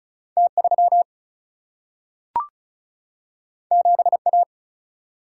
Morse code, heard as a keyed tone, about 36 wpm, 700 Hz. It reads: T 3 7 A
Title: Single-Letter Single-Number 35wpm